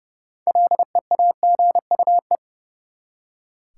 Morse code, heard as a keyed tone, about 32 wpm, 700 Hz. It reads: LEAGUE
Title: Six Letter Words - 30wpm